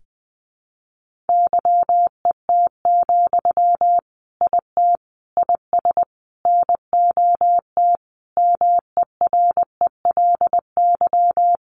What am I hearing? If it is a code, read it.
Morse code, heard as a keyed tone, about 20 wpm, 700 Hz. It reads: YET, IT IS NOT MERELY
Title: Sentences from Top 1400 Words 20wpm